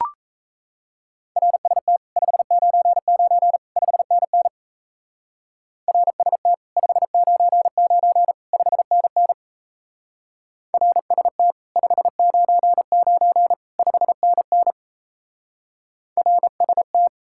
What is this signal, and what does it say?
Morse code, heard as a keyed tone, about 35 wpm, 700 Hz. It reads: RST 599 5NN RST 599 5NN RST 599 5NN RST